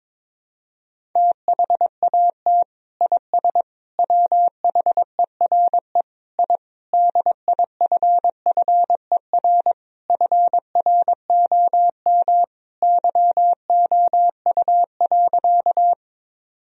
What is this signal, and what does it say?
Morse code, heard as a keyed tone, about 22 wpm, 700 Hz. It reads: THAT IS WHERE I DIFFER FROM YOU.
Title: Sentences from Top 1000 Words - Review Part 1 - 22wpm